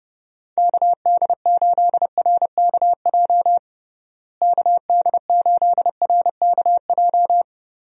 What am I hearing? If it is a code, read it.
Morse code, heard as a keyed tone, about 30 wpm, 700 Hz. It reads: KD8RKJ KD8RKJ
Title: USA Call Signs - 6 Characters - Repeated 3x - 30wpm